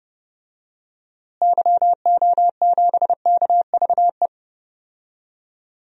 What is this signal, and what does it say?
Morse code, heard as a keyed tone, about 30 wpm, 700 Hz. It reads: YO7KVE
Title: DXCC-100 Call Signs - 6 Characters - Repeated 3x - 30wpm